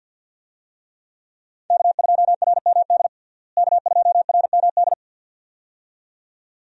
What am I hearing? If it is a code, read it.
Morse code, heard as a keyed tone, about 49 wpm, 700 Hz. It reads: K2RMD K2RMD
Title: USA Call Signs - 5 Characters - Repeated 2x - 50wpm